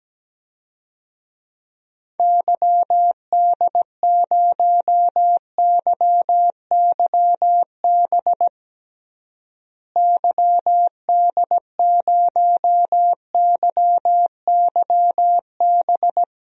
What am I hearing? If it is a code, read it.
Morse code, heard as a keyed tone, about 17 wpm, 700 Hz. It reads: YD0YYB YD0YYB
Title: DXCC-100 Call Signs - 6 Characters - Repeated 3x - 17wpm